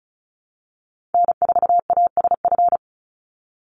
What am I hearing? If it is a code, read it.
Morse code, heard as a keyed tone, about 34 wpm, 700 Hz. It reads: N4ASF
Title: Call Signs - 35wpm